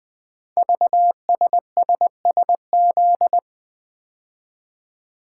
Morse code, heard as a keyed tone, about 20 wpm, 700 Hz. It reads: VSSSZ
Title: Code Groups - 5 Letters - 20wpm